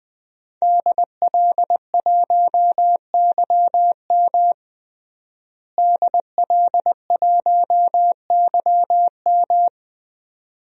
Morse code, heard as a keyed tone, about 20 wpm, 700 Hz. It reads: DL1YM DL1YM
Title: DXCC-100 Call Signs - 5 Characters - Repeated 2x - 20wpm